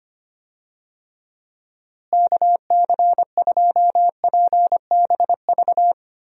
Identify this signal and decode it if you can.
Morse code, heard as a keyed tone, about 25 wpm, 700 Hz. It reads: KC2PBV
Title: USA Call Signs - 6 Characters - Repeated 3x - 25wpm